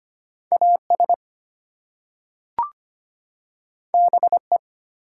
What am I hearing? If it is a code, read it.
Morse code, heard as a keyed tone, about 25 wpm, 700 Hz. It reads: AS BE
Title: Two Letter Words - 25wpm